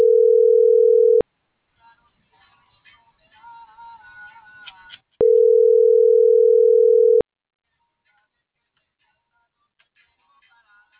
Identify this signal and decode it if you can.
Ringback tone (North America), 440 + 480 Hz.